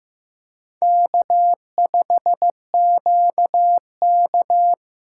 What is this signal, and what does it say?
Morse code, heard as a keyed tone, about 15 wpm, 700 Hz. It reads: K5QK
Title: USA Call Signs - 4 Characters - 2x Character Spacing - 15wpm